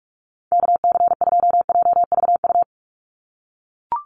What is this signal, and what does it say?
Morse code, heard as a keyed tone, about 44 wpm, 700 Hz. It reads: KC2JVU
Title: USA Call Signs 45wpm